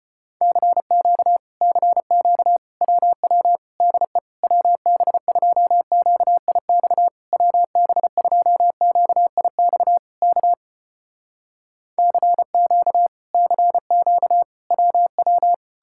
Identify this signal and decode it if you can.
Morse code, heard as a keyed tone, about 33 wpm, 700 Hz. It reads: CQ CQ WW DE WB2QIX WB2QIX K CQ CQ WW